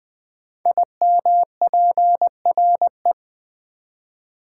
Morse code, heard as a keyed tone, about 20 wpm, 700 Hz. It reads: IMPRE
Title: Build A Word - 5 Letters At A Time - 10-Letter Words - 20wpm